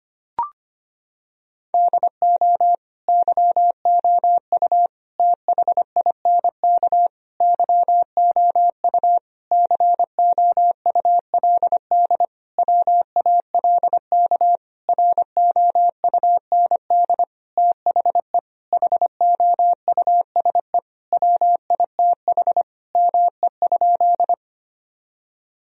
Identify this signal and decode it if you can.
Morse code, heard as a keyed tone, about 25 wpm, 700 Hz. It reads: DO YOU THINK YOU COULD WALK ROUND THE HOUSE WITH ME?
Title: Sentences from Top 1000 Words - Review Part 1 - 25wpm